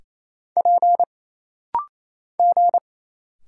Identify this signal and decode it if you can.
Morse code, heard as a keyed tone, about 28 wpm, 700 Hz. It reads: P G
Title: Single Letters - Rapid-Fire 28wpm